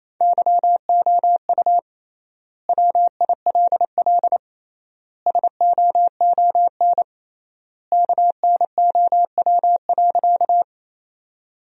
Morse code, heard as a keyed tone, about 28 wpm, 700 Hz. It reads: YOU WILL SOON KNOW.
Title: 3x Word-Spacing - Sentences from Top 400 Words - 28wpm